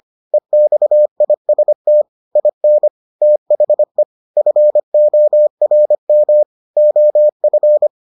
Morse code, heard as a keyed tone, about 25 wpm, 600 Hz. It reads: EXIST IN THE FORM OF